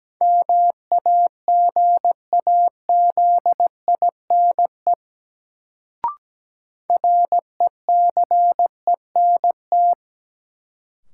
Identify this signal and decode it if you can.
Morse code, heard as a keyed tone, about 17 wpm, 700 Hz. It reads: MAGAZINE RECENT
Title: Sentences from Top 1000 Words - Review Part 7 - New Word Drill - 17wpm